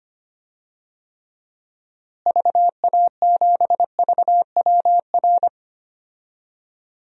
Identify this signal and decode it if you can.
Morse code, heard as a keyed tone, about 25 wpm, 700 Hz. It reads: VA7VWR
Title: DXCC-100 Call Signs - 6 Characters - Repeated 2x - 25wpm